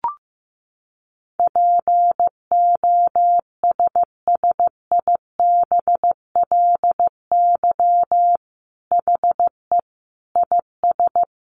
Morse code, heard as a keyed tone, about 15 wpm, 700 Hz. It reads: POSSIBLY HE IS
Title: Sentences from Top 1300 Words 15wpm